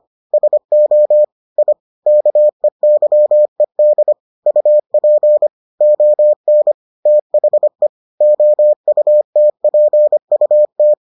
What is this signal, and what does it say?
Morse code, heard as a keyed tone, about 25 wpm, 600 Hz. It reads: SO I KEYED UP ON THE OUTPUT